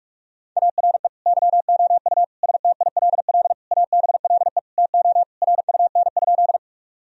Morse code, heard as a keyed tone, about 41 wpm, 700 Hz. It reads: ARE YOU STILL ABLE TO RUN?
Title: Sentences from Top 1000 Words - Review Part 2 - 45wpm